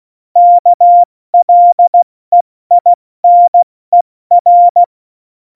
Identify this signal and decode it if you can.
Morse code, heard as a keyed tone, about 16 wpm, 700 Hz. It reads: KLEINER
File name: Morsekurs / Witze / Witz 29. Januar 2019, 70 F80